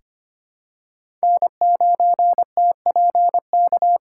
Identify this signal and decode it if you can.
Morse code, heard as a keyed tone, about 25 wpm, 700 Hz. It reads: N9TPK
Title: USA Call Signs - 2x Character Spacing - 25wpm